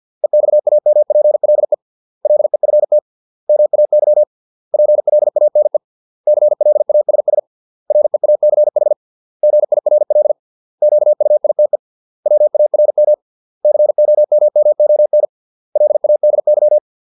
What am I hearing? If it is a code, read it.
Morse code, heard as a keyed tone, about 50 wpm, 600 Hz. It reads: EXAMPLE LEFT MAY PLANE CLASS REACH GIRL QUITE WARM COMMON LADY